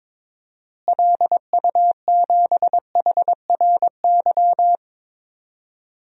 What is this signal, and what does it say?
Morse code, heard as a keyed tone, about 21 wpm, 700 Hz. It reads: LU7HRY
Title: DXCC-100 Call Signs - 6 Characters - Repeated 3x - 22wpm